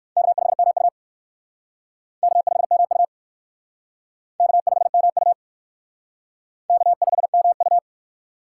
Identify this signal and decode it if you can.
Morse code, heard as a keyed tone, about 60 wpm, 700 Hz. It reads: K5MU K5MU K5MU K5MU